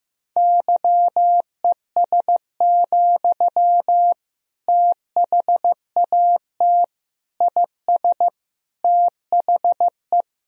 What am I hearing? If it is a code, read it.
Morse code, heard as a keyed tone, about 15 wpm, 700 Hz. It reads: YES, THAT IS THE
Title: Sentences from Top 1000 Words - Review Part 4 - 15wpm